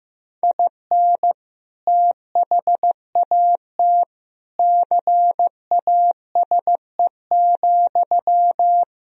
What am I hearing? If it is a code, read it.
Morse code, heard as a keyed tone, about 15 wpm, 700 Hz. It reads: IN THAT CASE,
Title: Sentences from Top 1000 Words - Review Part 1 - 15wpm